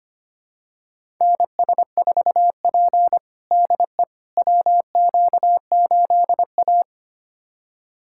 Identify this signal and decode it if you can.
Morse code, heard as a keyed tone, about 25 wpm, 700 Hz. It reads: NS4P DE WQ8A
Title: USA Callsign DE Callsign - 4 Characters - 25wpm